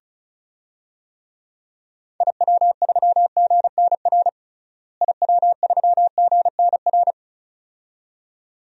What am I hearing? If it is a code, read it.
Morse code, heard as a keyed tone, about 36 wpm, 700 Hz. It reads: IW3GNR IW3GNR
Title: DXCC-100 Call Signs - 6 Characters - Repeated 2x - 35wpm